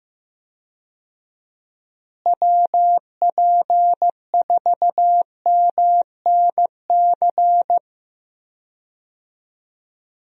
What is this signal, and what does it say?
Morse code, heard as a keyed tone, about 15 wpm, 700 Hz. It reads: WP4MNC
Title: USA Call Signs - 6 Characters - Repeated 2x - 15wpm